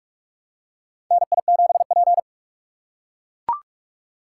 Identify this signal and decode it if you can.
Morse code, heard as a keyed tone, about 42 wpm, 700 Hz. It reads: NI7P
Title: USA Call Signs - 4 Characters - 6x Character Spacing - 45wpm